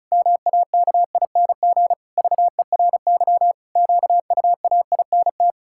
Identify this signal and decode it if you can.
Morse code, heard as a keyed tone, about 34 wpm, 700 Hz. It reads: MAKING VERY QUAINT
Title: Pangram Challenge v2 - 35wpm